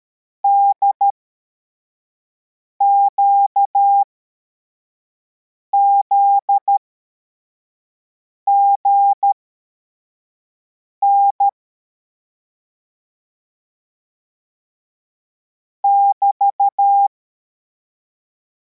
Morse code, heard as a keyed tone, about 13 wpm, spaced as for 4 wpm, 800 Hz. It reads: DQZGN <BT>
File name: formation / cw / Text73